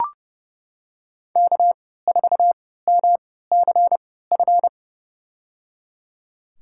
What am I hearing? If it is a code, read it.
Morse code, heard as a keyed tone, about 30 wpm, spaced as for 17 wpm, 700 Hz. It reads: K4MCF
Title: USA Call Signs - 5 Characters - 3x Character Spacing - 30wpm